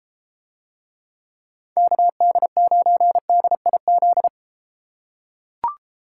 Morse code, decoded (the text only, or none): KD9DIZ